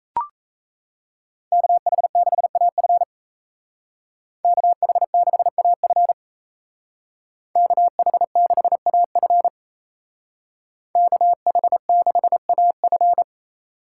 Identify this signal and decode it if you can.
Morse code, heard as a keyed tone, about 36 wpm, 700 Hz. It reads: KH6AF KH6AF KH6AF KH6AF